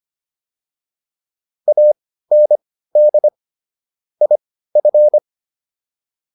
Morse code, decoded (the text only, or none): AND IF